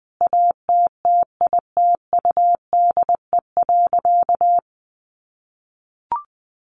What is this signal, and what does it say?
Morse code, heard as a keyed tone, about 20 wpm, 700 Hz. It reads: ATTITUDE.
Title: Sentences from Top 1300 Words 20wpm